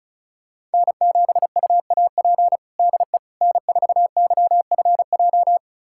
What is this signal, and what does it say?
Morse code, heard as a keyed tone, about 36 wpm, 700 Hz. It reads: N7UAP DE N4YFJ
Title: USA Callsign DE Callsign 35wpm